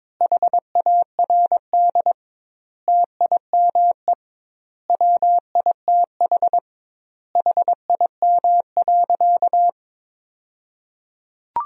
Morse code, heard as a keyed tone, about 21 wpm, 700 Hz. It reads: HARD TIME WITH HIM.
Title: 2x Word-Spacing - Sentences from Top 300 Words - 22wpm